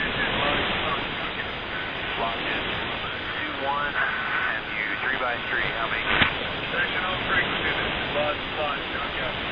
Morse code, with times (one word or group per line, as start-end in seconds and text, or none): none